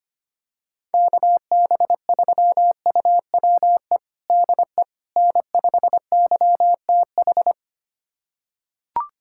Morse code, heard as a keyed tone, about 25 wpm, 700 Hz. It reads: KB3UWE DE N5YTH
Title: USA Callsign DE Callsign 25wpm